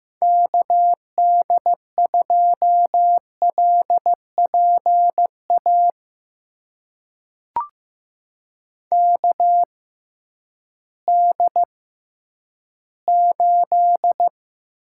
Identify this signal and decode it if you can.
Morse code, heard as a keyed tone, about 15 wpm, spaced as for 10 wpm, 700 Hz. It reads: KD2LPA KD8
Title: USA Call Signs - 6 Characters - 6x Character Spacing - 15wpm